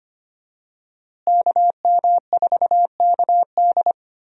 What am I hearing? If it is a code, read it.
Morse code, heard as a keyed tone, about 25 wpm, 700 Hz. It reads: KM4KD
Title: USA Call Signs - 5 Characters - Repeated 3x - 25wpm